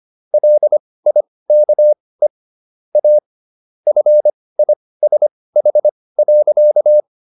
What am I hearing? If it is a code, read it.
Morse code, heard as a keyed tone, about 25 wpm, spaced as for 18 wpm, 600 Hz. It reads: LIKE A FISH.